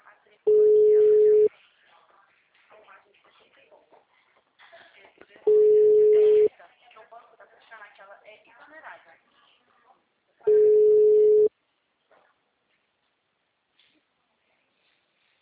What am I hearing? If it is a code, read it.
Ringback tone (Europe, 425 Hz), with speech.
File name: out- -2007- - - .96